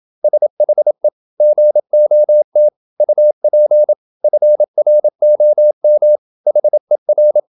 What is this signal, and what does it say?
Morse code, heard as a keyed tone, about 27 wpm, 600 Hz. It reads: SHE GOT UP FROM HER
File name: Never 0027wpm